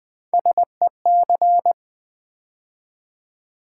Morse code, heard as a keyed tone, about 20 wpm, 700 Hz. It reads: SEC
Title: Build A Word - 3 Letters At A Time - 6-Letter Words - 20wpm